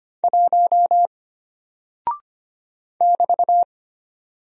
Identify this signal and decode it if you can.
Morse code, heard as a keyed tone, about 25 wpm, 700 Hz. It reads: 1 <BT>
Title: Single Character (Letter:Number:Punctuation:Prosign) Alt 25wpm